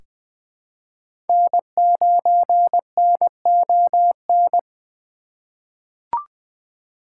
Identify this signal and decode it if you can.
Morse code, heard as a keyed tone, about 20 wpm, 700 Hz. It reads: N9NON
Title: USA Call Signs - 3x Character Spacing - 20wpm